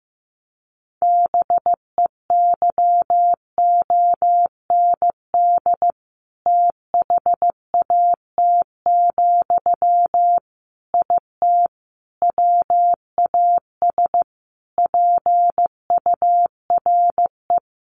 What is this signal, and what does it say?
Morse code, heard as a keyed tone, about 15 wpm, 700 Hz. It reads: BEYOND THAT, IT WAS PURE
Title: Sentences from Top 1500 Words 15wpm